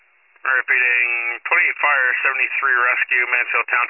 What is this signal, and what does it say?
Emergency Alert System (SAME) tones, heard as harsh buzzing bursts, with speech. It is heard from 0.3 to 3.0 s.